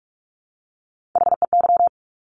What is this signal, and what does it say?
Morse code, heard as a keyed tone, about 46 wpm, 700 Hz. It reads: HEY
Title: Top Three Letter Combinations in Words 45wpm